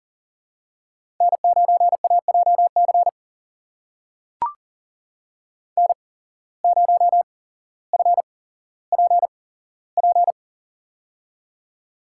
Morse code, heard as a keyed tone, about 40 wpm, spaced as for 17 wpm, 700 Hz. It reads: N9AJC N0FPP
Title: USA Call Signs - 5 Characters - 8x Character Spacing - 40wpm